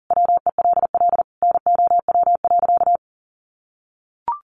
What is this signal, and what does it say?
Morse code, heard as a keyed tone, about 37 wpm, 700 Hz. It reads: WELL NOW.
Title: Sentences from Top 1000 Words 40wpm